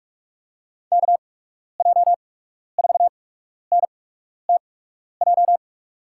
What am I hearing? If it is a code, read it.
Morse code, heard as a keyed tone, about 46 wpm, spaced as for 12 wpm, 700 Hz. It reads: KJ4NTJ
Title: USA Call Signs - 8x Character Spacing - 45wpm